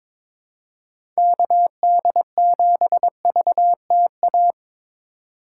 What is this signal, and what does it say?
Morse code, heard as a keyed tone, about 22 wpm, 700 Hz. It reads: KD7VTA
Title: USA Call Signs - 3x Character Spacing - 22wpm